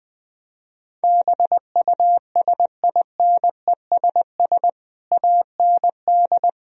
Morse code, heard as a keyed tone, about 20 wpm, 700 Hz. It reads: BUSINESS AND
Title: Sentences from Top 1000 Words - Review Part 3 - 20wpm